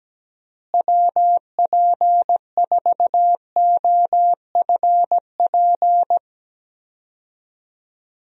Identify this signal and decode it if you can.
Morse code, heard as a keyed tone, about 17 wpm, 700 Hz. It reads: WP4OFP
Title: USA Call Signs - 6 Characters - Repeated 2x - 17wpm